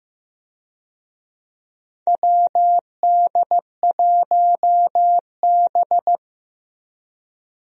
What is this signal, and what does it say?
Morse code, heard as a keyed tone, about 15 wpm, 700 Hz. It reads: WD1B
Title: USA Call Signs - 4 Characters - Repeated 2x - 15wpm